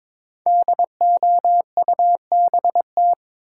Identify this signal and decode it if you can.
Morse code, heard as a keyed tone, about 23 wpm, 700 Hz. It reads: DOUBT
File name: 5 Letter Words - First 3 Letters Given - 22wpm